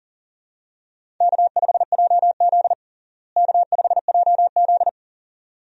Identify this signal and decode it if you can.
Morse code, heard as a keyed tone, about 40 wpm, 700 Hz. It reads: K5JZ K5JZ
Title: USA Call Signs - 4 Characters - Repeated 2x - 40wpm